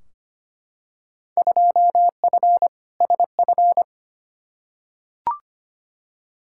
Morse code, heard as a keyed tone, about 25 wpm, 700 Hz. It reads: 2F SF
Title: Field Day Exchanges - 25wpm